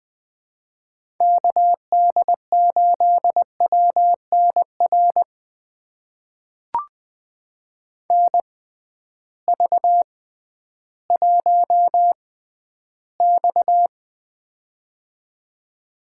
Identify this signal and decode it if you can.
Morse code, heard as a keyed tone, about 20 wpm, spaced as for 12 wpm, 700 Hz. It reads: KD8WNR NV1X
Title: USA Call Signs - 6x Character Spacing - 20wpm